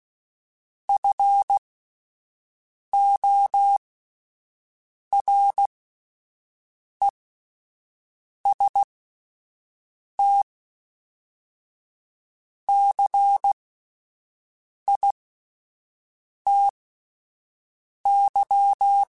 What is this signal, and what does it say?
Morse code, heard as a keyed tone, about 16 wpm, spaced as for 5 wpm, 800 Hz. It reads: FORESTCITY
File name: fromSMs / FromSMs mp3s / Cities@5